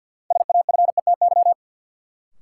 Morse code, heard as a keyed tone, about 52 wpm, 700 Hz. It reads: SAFETY
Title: Sentences from Top 700 Words - New Word Drill - 50wpm